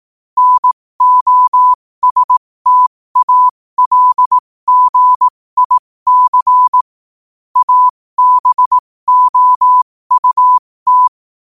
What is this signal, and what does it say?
Morse code, heard as a keyed tone, about 18 wpm, spaced as for 15 wpm, 1000 Hz. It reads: NOSTALGIC ABOUT
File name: qotdfiles / 15wpm